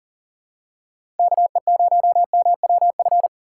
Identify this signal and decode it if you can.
Morse code, heard as a keyed tone, about 40 wpm, 700 Hz. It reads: KE0MWF
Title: USA Call Signs - 6 Characters - 4x Character Spacing - 40wpm